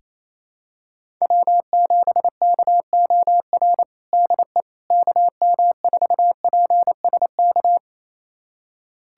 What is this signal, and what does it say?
Morse code, heard as a keyed tone, about 28 wpm, 700 Hz. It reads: W7KOR DE KM4PSK
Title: USA Callsign DE Callsign 28wpm